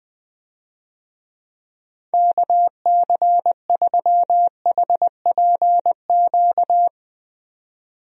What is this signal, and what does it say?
Morse code, heard as a keyed tone, about 21 wpm, 700 Hz. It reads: KC3HPQ
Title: USA Call Signs - 6 Characters - 20wpm